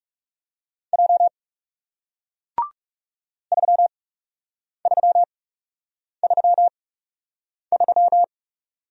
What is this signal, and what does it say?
Morse code, heard as a keyed tone, about 37 wpm, spaced as for 8 wpm, 700 Hz. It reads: J 3333
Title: Speed-Racing Single Letter-Number 30wpm